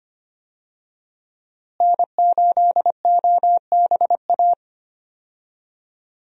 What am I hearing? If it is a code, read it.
Morse code, heard as a keyed tone, about 25 wpm, 700 Hz. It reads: N8OBA
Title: USA Call Signs - 5 Characters - Repeated 3x - 25wpm